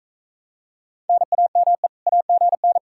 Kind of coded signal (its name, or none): Morse code